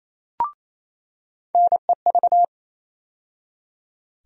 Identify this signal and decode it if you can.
Morse code, heard as a keyed tone, about 28 wpm, 700 Hz. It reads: NEV
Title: USA Call Sign Postfixes - 3 Letters 28wpm